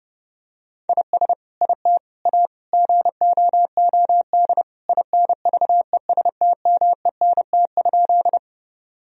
Morse code, heard as a keyed tone, about 30 wpm, 700 Hz. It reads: IS IT A GOOD INVESTMENT?
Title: Sentences from Top 1000 Words - Review Part 4 - 30wpm